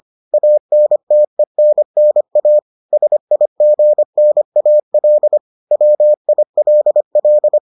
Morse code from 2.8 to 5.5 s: SIGNAL

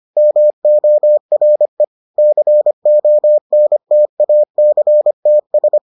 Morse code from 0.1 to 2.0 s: MORE

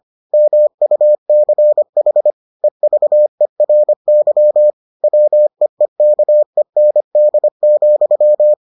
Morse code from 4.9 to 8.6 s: WEEKEND,

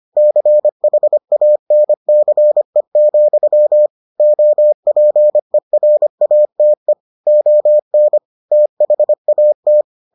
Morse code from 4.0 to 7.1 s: OPERATE

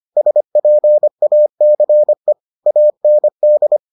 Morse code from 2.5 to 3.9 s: AND

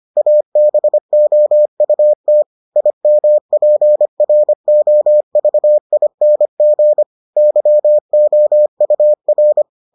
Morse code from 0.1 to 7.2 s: ABOUT IMPROVING

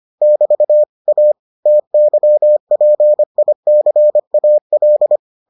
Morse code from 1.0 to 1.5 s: A

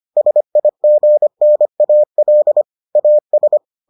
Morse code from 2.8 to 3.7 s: AS

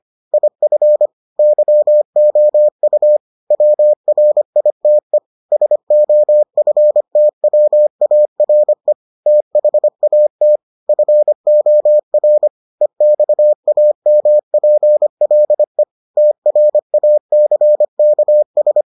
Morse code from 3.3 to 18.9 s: WRITE SOFTWARE THAT FOR EXAMPLE TRACKS

